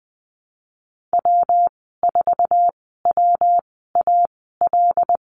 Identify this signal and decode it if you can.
Morse code, heard as a keyed tone, about 19 wpm, spaced as for 14 wpm, 700 Hz. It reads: W4WAL
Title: 140 Lesson 23g - Letter f - Call Signs 20wpm